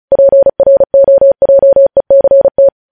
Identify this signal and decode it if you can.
Morse code, heard as a keyed tone, about 36 wpm, 550 Hz. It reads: PROJECT
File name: podcast / News headlines-1x-35x35